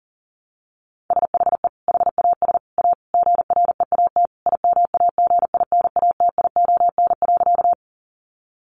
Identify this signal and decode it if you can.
Morse code, heard as a keyed tone, about 40 wpm, 700 Hz. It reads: SHE HAS A GREAT IMAGINATION.
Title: Sentences from Top 1300 Words 40wpm